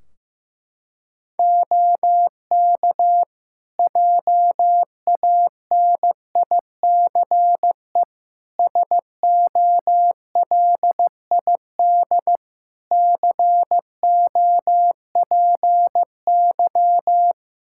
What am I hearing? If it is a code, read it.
Morse code, heard as a keyed tone, about 16 wpm, 700 Hz. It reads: OK JANICE SOLID COPY
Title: Practice QSO - Exchange 3 - More Info - 15wpm